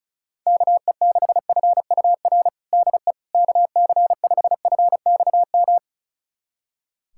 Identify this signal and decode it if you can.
Morse code, heard as a keyed tone, about 36 wpm, 690 Hz. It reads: KE6FUR DE KC5FXM
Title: USA Callsign DE Callsign 35wpm